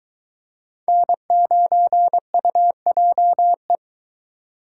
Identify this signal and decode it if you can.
Morse code, heard as a keyed tone, about 22 wpm, 700 Hz. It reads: N9UJE